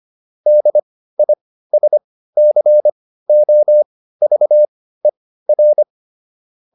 Morse code, heard as a keyed tone, about 25 wpm, spaced as for 15 wpm, 600 Hz. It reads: DISCOVER